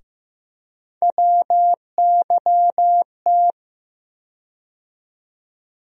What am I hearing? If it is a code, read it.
Morse code, heard as a keyed tone, about 15 wpm, 700 Hz. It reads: WYT
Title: USA Call Sign Postfixes - 3 Letters 15wpm